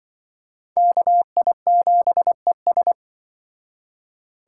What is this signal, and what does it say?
Morse code, heard as a keyed tone, about 24 wpm, 700 Hz. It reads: KI7ES
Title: Speed Racing - USA 2x2 Call Signs 20wpm